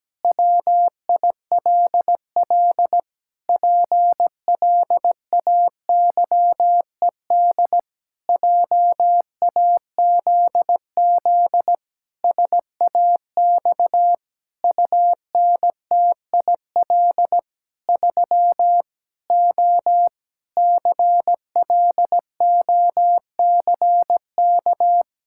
Morse code, heard as a keyed tone, about 17 wpm, 700 Hz. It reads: WILL PLAYED JAZZ SAX UNTIL 3 O CLOCK
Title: Pangram Challenge v2 - 17wpm